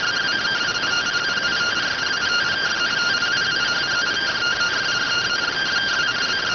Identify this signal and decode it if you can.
Two-tone teleprinter signal, about 170 Hz shift.